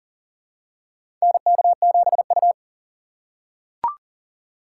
Morse code, heard as a keyed tone, about 40 wpm, 700 Hz. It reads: NK7U
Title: K1USN SST QSO Practice - 40wpm